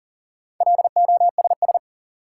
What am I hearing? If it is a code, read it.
Morse code, heard as a keyed tone, about 37 wpm, 700 Hz. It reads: LOSS
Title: Sentences from Top 1400 Words - New Word Drill Alt - 40wpm